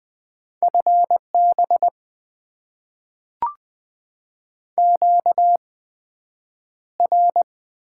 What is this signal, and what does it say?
Morse code, heard as a keyed tone, about 20 wpm, spaced as for 9 wpm, 700 Hz. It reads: FB QR
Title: Top 90 QSO Elements - 8x Character Spacing - 20wpm